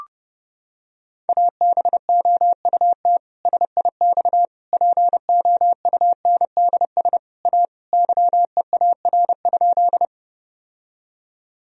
Morse code, heard as a keyed tone, about 30 wpm, 700 Hz. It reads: ABOUT SIX POUNDS A YEAR?
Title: Sentences from Top 1000 Words - Review Part 2 - 30wpm